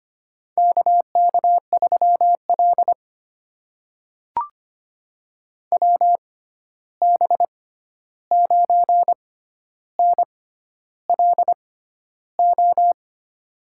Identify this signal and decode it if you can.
Morse code, heard as a keyed tone, about 25 wpm, spaced as for 11 wpm, 700 Hz. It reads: KK3L WB9NLO
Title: USA Call Signs - 6x Character Spacing - 25wpm